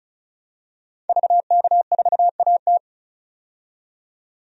Morse code, heard as a keyed tone, about 34 wpm, 700 Hz. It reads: VK4AT
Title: DXCC-100 Call Signs - 5 Characters - 35wpm